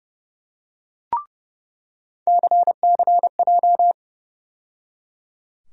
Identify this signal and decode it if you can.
Morse code, heard as a keyed tone, about 30 wpm, 700 Hz. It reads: CCJ
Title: USA Call Sign Postfixes - 3 Letters 30wpm